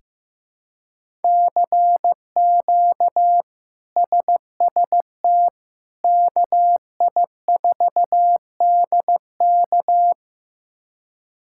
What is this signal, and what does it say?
Morse code, heard as a keyed tone, about 15 wpm, 700 Hz. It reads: CQ SST KI4DK
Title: K1USN SST QSO Practice - 15wpm